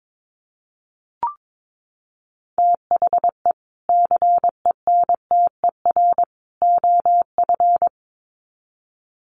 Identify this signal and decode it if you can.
Morse code, heard as a keyed tone, about 22 wpm, 700 Hz. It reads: THE CENTER OF
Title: Sets of 3 Words 22wpm